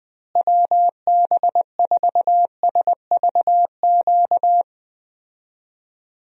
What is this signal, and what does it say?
Morse code, heard as a keyed tone, about 20 wpm, 700 Hz. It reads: WB4SVQ
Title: USA Call Signs - 6 Characters - Repeated 3x - 20wpm